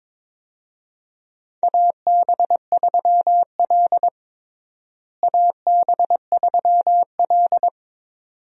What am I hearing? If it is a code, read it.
Morse code, heard as a keyed tone, about 21 wpm, 700 Hz. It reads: AB3L AB3L
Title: USA Call Signs - 4 Characters - Repeated 2x - 22wpm